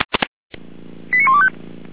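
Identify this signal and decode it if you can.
Five-tone selective call (ZVEI), heard as quick runs of five tones, tones of about 70 ms. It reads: 89125
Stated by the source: ZVEI five-tone selective call received on an SDR.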